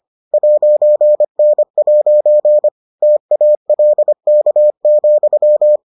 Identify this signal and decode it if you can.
Morse code, heard as a keyed tone, about 25 wpm, 600 Hz. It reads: 'N' TALK,